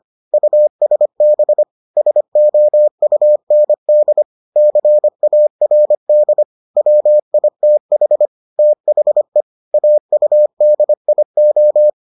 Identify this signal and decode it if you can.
Morse code, heard as a keyed tone, about 25 wpm, 600 Hz. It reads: USB SOUND CARD WITH THE AUDIO